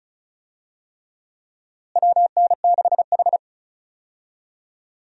Morse code, heard as a keyed tone, about 34 wpm, 700 Hz. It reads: WN6H
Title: USA Call Signs - 4 Characters - Repeated 3x - 35wpm